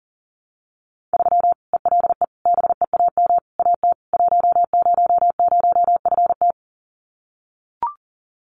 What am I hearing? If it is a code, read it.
Morse code, heard as a keyed tone, about 38 wpm, 700 Hz. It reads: 3 ELE BEAM AT 100FT
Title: Common QSO Phrases 40wpm